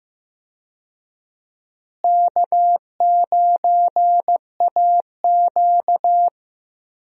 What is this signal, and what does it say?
Morse code, heard as a keyed tone, about 16 wpm, 700 Hz. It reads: K9AQ
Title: USA Call Signs - 4 Characters - Repeated 2x - 15wpm